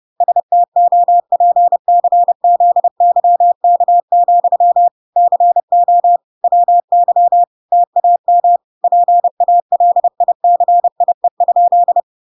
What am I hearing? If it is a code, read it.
Morse code, heard as a keyed tone, about 30 wpm, 700 Hz. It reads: STOPCZYK, CO WY TAM PALICIE?